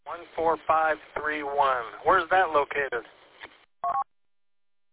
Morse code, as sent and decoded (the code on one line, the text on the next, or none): none